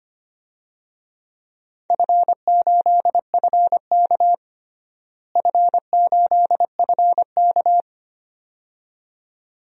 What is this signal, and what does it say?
Morse code, heard as a keyed tone, about 25 wpm, 700 Hz. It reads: F8FK F8FK
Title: DXCC-100 Call Signs - 4 Characters - Repeated 2x - 25wpm